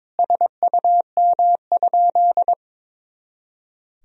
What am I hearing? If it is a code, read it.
Morse code, heard as a keyed tone, about 22 wpm, 700 Hz. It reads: SUM?
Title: Sentences from Top 100 Words and Abbreviations 22wpm